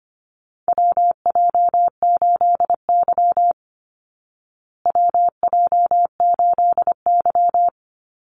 Morse code, heard as a keyed tone, about 25 wpm, 700 Hz. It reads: WJ8Y WJ8Y
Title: Call Signs Repeated 2x - 25wpm